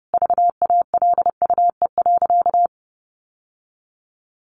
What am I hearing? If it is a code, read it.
Morse code, heard as a keyed tone, about 30 wpm, 700 Hz. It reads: VALUE.
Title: Sentences from Top 600 Words 30wpm